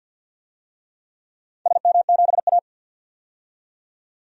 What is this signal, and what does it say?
Morse code, heard as a keyed tone, about 53 wpm, 700 Hz. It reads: SM7A
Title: DXCC-100 Call Signs - 4 Characters - 50wpm